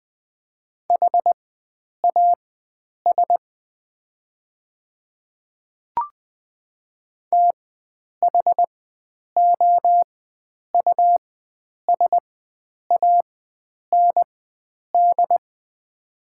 Morse code, heard as a keyed tone, about 20 wpm, spaced as for 9 wpm, 700 Hz. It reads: HAS THOUSAND
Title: 4x Character Spacing - Top 500 Words - 20wpm